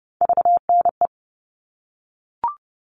Morse code, heard as a keyed tone, about 30 wpm, 700 Hz. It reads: VNE
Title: All Cut Numbers - 3-Digit 30wpm